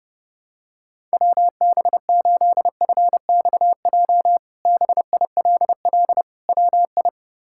Morse code, heard as a keyed tone, about 31 wpm, 700 Hz. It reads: WB8FXJ BILL WI
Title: K1USN SST QSO Practice - 30wpm